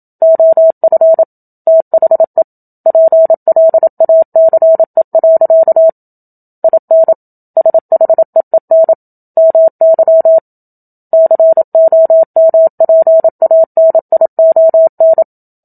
Morse code, heard as a keyed tone, about 27 wpm, 650 Hz. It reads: OF THE PLACE. IN SHEEN MY COMPANION
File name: Book Two / 01-Chapter 1 - 27 WPM